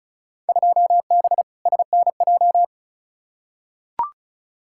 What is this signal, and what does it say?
Morse code, heard as a keyed tone, about 36 wpm, 700 Hz. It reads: 2B SNJ
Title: Field Day Exchanges Encore - 35wpm